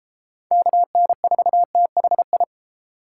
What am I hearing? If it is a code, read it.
Morse code, heard as a keyed tone, about 30 wpm, 700 Hz. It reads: KN4THI